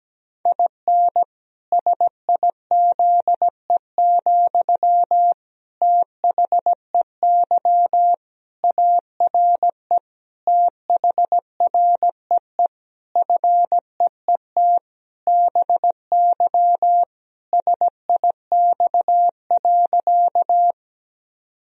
Morse code, heard as a keyed tone, about 17 wpm, 700 Hz. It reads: IN SIZE, THEY ARE THREE FEET BY SIX.
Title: Sentences from Top 1000 Words - Review Part 2 - 17wpm